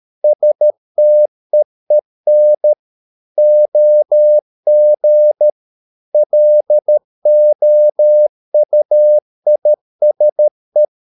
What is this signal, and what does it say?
Morse code, heard as a keyed tone, about 13 wpm, 600 Hz. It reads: STEEN OG LOUISE
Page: Ferie 13 wpm